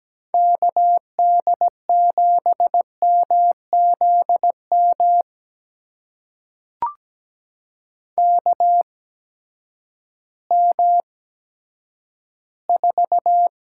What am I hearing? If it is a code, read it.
Morse code, heard as a keyed tone, about 17 wpm, spaced as for 10 wpm, 700 Hz. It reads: KD7MZM KM4
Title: USA Call Signs - 8x Character Spacing - 17wpm